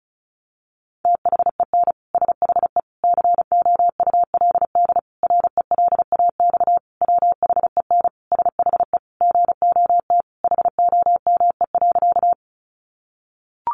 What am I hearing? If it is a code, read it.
Morse code, heard as a keyed tone, about 34 wpm, 700 Hz. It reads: THEN SHE COULD RELAX WHEN SHE GOT HOME.
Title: Sentences from Top 1400 Words 35wpm